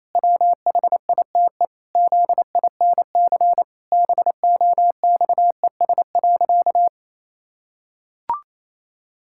Morse code, heard as a keyed tone, about 28 wpm, 700 Hz. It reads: WHITE ZINC BOXES.
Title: Pangram Challenge v2 - 28wpm